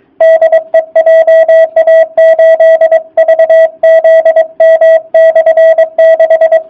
Morse code, heard as a keyed tone, about 22 wpm, 660 Hz. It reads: DE JA8VZM/6
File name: jh8cbh / 2019cw / sp24- w